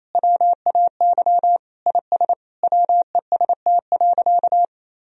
Morse code, heard as a keyed tone, about 28 wpm, 700 Hz. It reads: WAY IS WEST.
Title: Sentences from Top 1000 Words - Review Part 3 - 28wpm